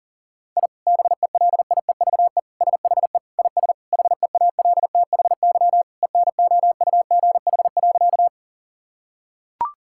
Morse code, heard as a keyed tone, about 40 wpm, 700 Hz. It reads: I BELIEVE SHE IS HEALTHY ENOUGH.
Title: Sentences from Top 1000 Words - Review Part 5 - 40wpm